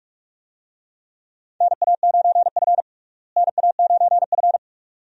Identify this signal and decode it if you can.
Morse code, heard as a keyed tone, about 46 wpm, 700 Hz. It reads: NA9F NA9F
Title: USA Call Signs - 4 Characters - Repeated 2x - 45wpm